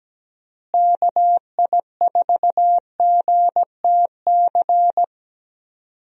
Morse code, heard as a keyed tone, about 17 wpm, 700 Hz. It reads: KI4GTC
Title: USA Call Signs - 6 Characters - Repeated 2x - 17wpm